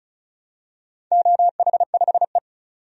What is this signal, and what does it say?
Morse code, heard as a keyed tone, about 36 wpm, 700 Hz. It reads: OH5E
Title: DXCC-100 Call Signs - 4 Characters - 35wpm